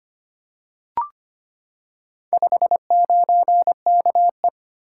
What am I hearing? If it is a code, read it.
Morse code, heard as a keyed tone, about 25 wpm, 700 Hz. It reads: 59KE
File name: practice / 25wpm / Four Letter-Number Combinations - 25wpm